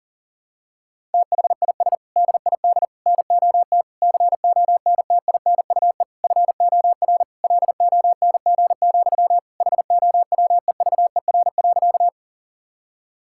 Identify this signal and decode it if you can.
Morse code, heard as a keyed tone, about 40 wpm, 700 Hz. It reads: THIS DID NOT CONTINUE FOR LONG, HOWEVER.
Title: Sentences from Top 1000 Words - Review Part 6 - 40wpm